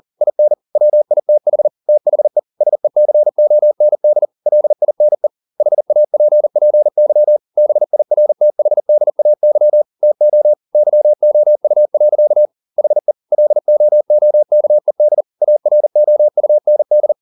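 Morse code, heard as a keyed tone, about 40 wpm, 600 Hz. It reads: IN WITH THE SECOND LINE HAPPY BIRTHDAY TO YOU. HE LOOKED AROUND
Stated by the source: Morse code at 40 wpm, practice text, CW med Gnister.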